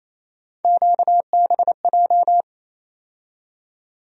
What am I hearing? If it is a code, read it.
Morse code, heard as a keyed tone, about 28 wpm, 700 Hz. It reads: QBJ
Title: USA Call Sign Postfixes - 3 Letters 28wpm